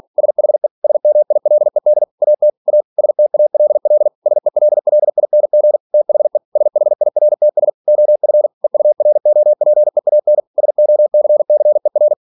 Morse code, heard as a keyed tone, about 46 wpm, 600 Hz. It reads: SHE SMILED AT A STALL SELLING THE SHIRTS OF EUROPEAN SOCCER
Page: Never 47wpm